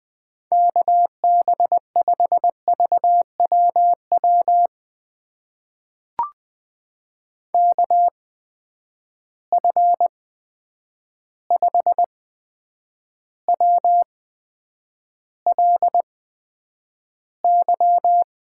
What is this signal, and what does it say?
Morse code, heard as a keyed tone, about 20 wpm, spaced as for 9 wpm, 700 Hz. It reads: KB5VWW KF5WLY
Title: USA Call Signs - 6 Characters - 8x Character Spacing - 20wpm